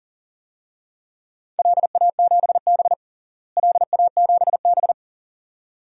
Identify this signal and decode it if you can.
Morse code, heard as a keyed tone, about 40 wpm, 700 Hz. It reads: LA7B LA7B
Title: DXCC-100 Call Signs - 4 Characters - Repeated 2x - 40wpm